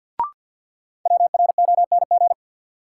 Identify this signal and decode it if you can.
Morse code, heard as a keyed tone, about 49 wpm, 700 Hz. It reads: WRONG
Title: Sets of 5 Words Encore - New Word Drill - 50wpm